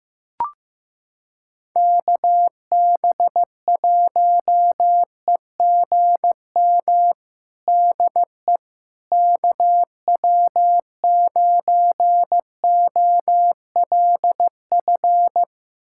Morse code, heard as a keyed tone, about 15 wpm, 700 Hz. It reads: KB1EGM DE KW9OLF
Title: USA Callsign DE Callsign 15wpm